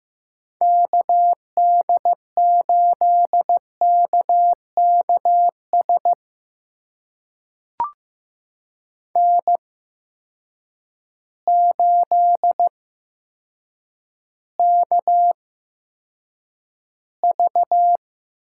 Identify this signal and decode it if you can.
Morse code, heard as a keyed tone, about 15 wpm, spaced as for 8 wpm, 700 Hz. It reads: KD8KKS N8KV
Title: USA Call Signs - 8x Character Spacing - 15wpm